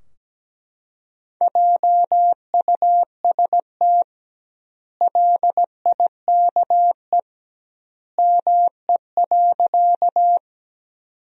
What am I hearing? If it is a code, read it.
Morse code, heard as a keyed tone, about 17 wpm, 700 Hz. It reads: JUST LIKE ME.
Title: 2x Word-Spacing - Sentences from Top 200 Words - 17wpm